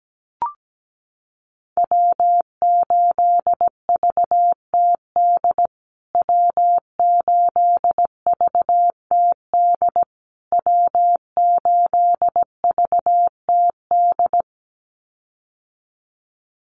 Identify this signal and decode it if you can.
Morse code, heard as a keyed tone, about 17 wpm, 700 Hz. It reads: W8VTD W8VTD W8VTD
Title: USA Call Signs Repeated 3x 17wpm